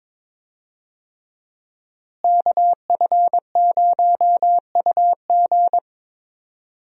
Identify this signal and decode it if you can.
Morse code, heard as a keyed tone, about 22 wpm, 700 Hz. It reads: KF0UG
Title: USA Call Signs - 5 Characters - Repeated 3x - 22wpm